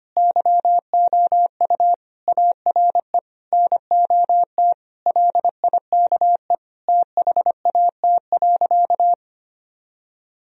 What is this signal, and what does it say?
Morse code, heard as a keyed tone, about 25 wpm, 700 Hz. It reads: YOU ARE NOT LIKE THAT.
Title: Sentences from Top 1000 Words - Review Part 3 - 25wpm